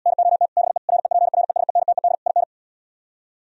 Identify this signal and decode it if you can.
Morse code, heard as a keyed tone, about 77 wpm, 700 Hz. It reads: NOT BE REPRESENTED IN